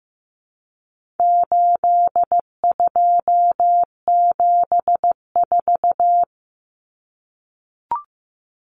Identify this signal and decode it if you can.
Morse code, heard as a keyed tone, about 15 wpm, 700 Hz. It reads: 8274
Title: Numbers - 4-Digit 15wpm